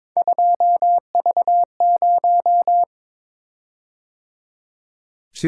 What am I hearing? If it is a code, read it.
Morse code, heard as a keyed tone, about 22 wpm, 700 Hz. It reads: 2V0
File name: practice / 22wpm / Three Letter-Number Combinations - 22wpm